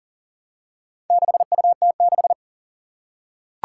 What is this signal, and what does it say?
Morse code, heard as a keyed tone, about 40 wpm, 700 Hz. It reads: 6UT6
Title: Four Challenging Letter-Number Combinations 40wpm